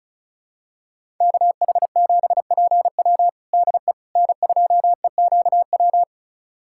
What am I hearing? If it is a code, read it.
Morse code, heard as a keyed tone, about 36 wpm, 700 Hz. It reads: KH7PW DE N2EQW
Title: USA Callsign DE Callsign - 5 Characters - 35wpm